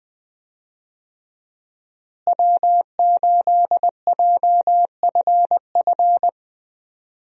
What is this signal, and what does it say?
Morse code, heard as a keyed tone, about 20 wpm, 700 Hz. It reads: W8JFF
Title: USA Call Signs - 5 Characters - Repeated 3x - 20wpm